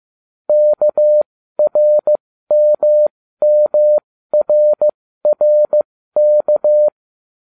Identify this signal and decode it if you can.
Morse code, heard as a keyed tone, about 15 wpm, spaced as for 13 wpm, 600 Hz. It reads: KRMMRRK